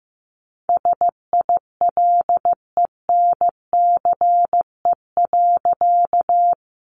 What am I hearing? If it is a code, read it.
Morse code, heard as a keyed tone, about 15 wpm, 700 Hz. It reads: SILENCE.
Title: A Sound Of Thunder 15wpm